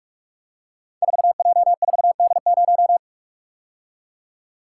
Morse code, heard as a keyed tone, about 44 wpm, 700 Hz. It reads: 4J4D0
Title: Code Groups - 5 Letter:Numbers - 45wpm